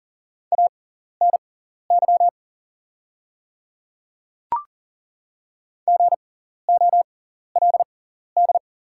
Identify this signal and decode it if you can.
Morse code, heard as a keyed tone, about 40 wpm, spaced as for 14 wpm, 690 Hz. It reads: ANY GOLD
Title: 6x Character Spacing - Top 500 Words - 40wpm